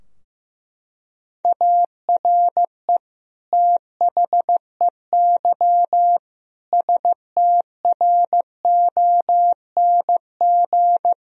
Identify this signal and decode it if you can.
Morse code, heard as a keyed tone, about 15 wpm, 700 Hz. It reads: ARE THEY STRONG
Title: Sentences from Top 1000 Words - Review Part 3 - 15wpm